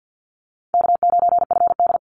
Morse code, heard as a keyed tone, about 50 wpm, 700 Hz. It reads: K9FD
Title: Easy Call Signs Repeated 2x - 50wpm